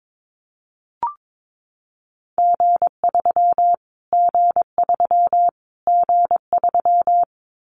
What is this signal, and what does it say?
Morse code, heard as a keyed tone, about 22 wpm, 700 Hz. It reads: G3 G3 G3
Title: Callsign Prefixes for Top 25 DXCC - Repeated 3x - 22wpm